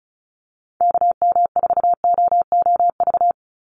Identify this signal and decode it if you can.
Morse code, heard as a keyed tone, about 34 wpm, 700 Hz. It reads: KM4OOV
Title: USA Call Signs 35wpm